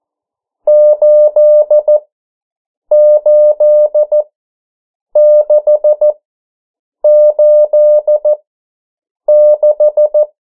Morse code, heard as a keyed tone, about 14 wpm, spaced as for 7 wpm, 600 Hz. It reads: 88686
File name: morse mp3 / 09 urok 8